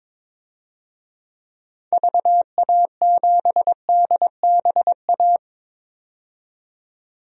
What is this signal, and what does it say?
Morse code, heard as a keyed tone, about 21 wpm, 700 Hz. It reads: VA7DBA
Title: DXCC-100 Call Signs - 6 Characters - 22wpm